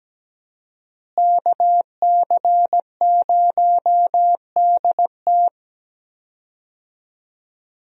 Morse code, heard as a keyed tone, about 17 wpm, 700 Hz. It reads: KC0DT
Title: USA Call Signs - 5 Characters - Repeated 3x - 17wpm